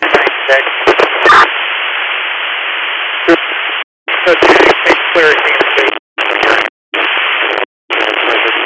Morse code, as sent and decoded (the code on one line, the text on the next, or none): none